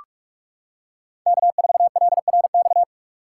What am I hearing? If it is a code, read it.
Morse code, heard as a keyed tone, about 43 wpm, 700 Hz. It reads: K4LRX
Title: K1USN SST QSO Practice - 45wpm